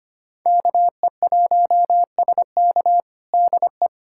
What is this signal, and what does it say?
Morse code, heard as a keyed tone, about 25 wpm, 700 Hz. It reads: KE1SK DE
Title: USA Callsign DE Callsign 25wpm